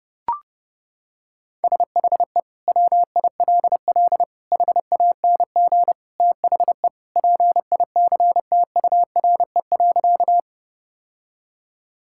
Morse code, heard as a keyed tone, about 30 wpm, 700 Hz. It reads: SHE WILL HANG THE PICTURE.
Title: Sentences from Top 1600 Words 30wpm